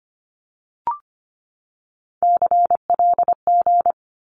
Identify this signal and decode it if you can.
Morse code, heard as a keyed tone, about 25 wpm, 700 Hz. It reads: CLG
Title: Abbreviations 25wpm